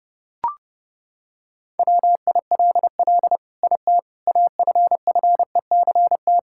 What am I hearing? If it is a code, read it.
Morse code, heard as a keyed tone, about 28 wpm, 700 Hz. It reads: WILL IT AFFECT
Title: Sentences from Top 1000 Words - Review Part 6 - 30wpm